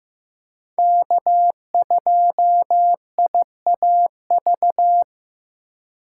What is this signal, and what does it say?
Morse code, heard as a keyed tone, about 15 wpm, 700 Hz. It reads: K2IAV
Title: USA Call Signs - 5 Characters - 3x Character Spacing - 15wpm